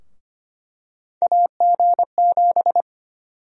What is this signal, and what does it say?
Morse code, heard as a keyed tone, about 25 wpm, 700 Hz. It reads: AG7
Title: USA Call Sign Prefixes - 25wpm